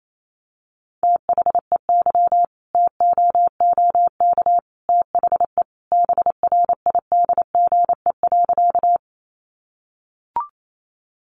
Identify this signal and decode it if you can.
Morse code, heard as a keyed tone, about 28 wpm, 700 Hz. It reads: THEY TOOK THE BRIDGE.
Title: Sentences from Top 1200 Words 28wpm